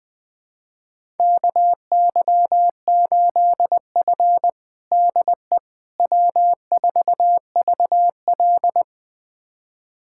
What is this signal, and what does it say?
Morse code, heard as a keyed tone, about 20 wpm, 700 Hz. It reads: KY8F DE W4VL
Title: USA Callsign DE Callsign - 4 Characters - 20wpm